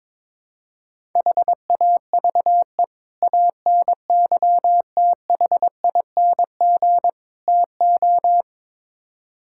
Morse code, heard as a keyed tone, about 22 wpm, 700 Hz. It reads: HAVE ANYTHING TO
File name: practice / 22wpm / Build A Phrase - 5-Word Phrases - 22wpm